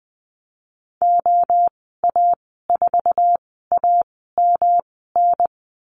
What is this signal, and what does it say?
Morse code, heard as a keyed tone, about 20 wpm, spaced as for 15 wpm, 700 Hz. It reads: OA4AMN
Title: 147 Lesson 24g - Letter y - Call Signs 20wpm